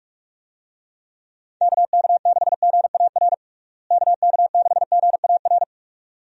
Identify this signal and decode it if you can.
Morse code, heard as a keyed tone, about 41 wpm, 700 Hz. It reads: KK6GAR KK6GAR
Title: USA Call Signs - 6 Characters - Repeated 2x - 45wpm